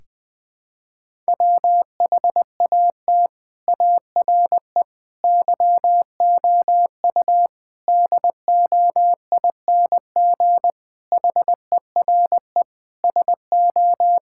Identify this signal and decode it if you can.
Morse code, heard as a keyed tone, about 19 wpm, 700 Hz. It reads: WHAT ARE YOU DOING HERE SO
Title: Sentences with 5-Word Phrases - 20wpm